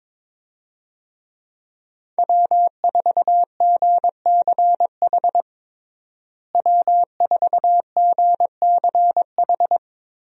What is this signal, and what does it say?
Morse code, heard as a keyed tone, about 22 wpm, 700 Hz. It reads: W4GCH W4GCH
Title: USA Call Signs - 5 Characters - Repeated 3x - 22wpm